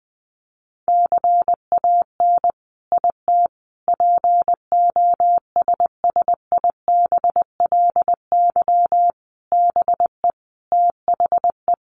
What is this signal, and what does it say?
Morse code, heard as a keyed tone, about 20 wpm, 700 Hz. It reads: CAN IT POSSIBLY BE THE
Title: Sentences from Top 1300 Words 20wpm